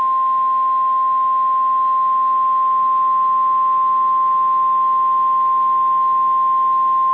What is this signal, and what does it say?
Emergency Alert System (SAME) tones (the 1050 Hz attention tone). It is heard from 0.7 to 6.7 s.